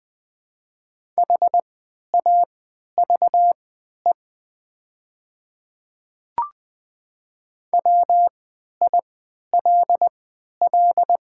Morse code, heard as a keyed tone, about 20 wpm, spaced as for 11 wpm, 700 Hz. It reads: HAVE WILL
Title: 3x Character Spacing - Top 100 Words - 20wpm